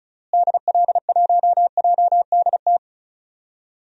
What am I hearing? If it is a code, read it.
Morse code, heard as a keyed tone, about 36 wpm, 700 Hz. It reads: DL1JDT
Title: DXCC-100 Call Signs - 6 Characters - Repeated 3x - 35wpm